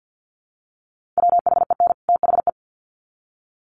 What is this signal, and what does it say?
Morse code, heard as a keyed tone, about 50 wpm, 700 Hz. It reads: WHEN THE
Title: Sets of 2 Words 50wpm